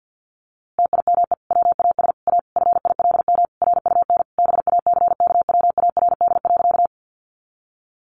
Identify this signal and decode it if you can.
Morse code, heard as a keyed tone, about 50 wpm, 700 Hz. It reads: TIME WAS A FILM RUN BACKWARD.
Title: A Sound Of Thunder 50wpm